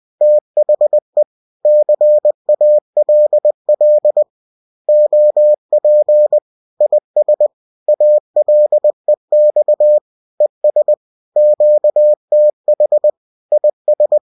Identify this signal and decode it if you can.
Morse code, heard as a keyed tone, about 20 wpm, 600 Hz. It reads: THE CALL OP IS ALEX ES QTH IS